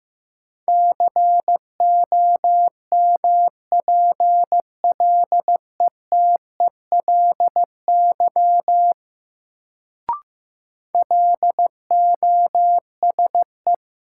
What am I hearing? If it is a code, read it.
Morse code, heard as a keyed tone, about 15 wpm, 700 Hz. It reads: COMPLETELY LOSE
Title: Sentences from Top 1100 Words - New Word Drill Alt - 15wpm